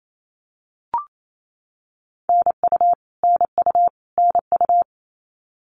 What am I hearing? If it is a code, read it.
Morse code, heard as a keyed tone, about 28 wpm, 700 Hz. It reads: NU NU NU
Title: Canadian Provinces and Territories - Repeated 3x - 28wpm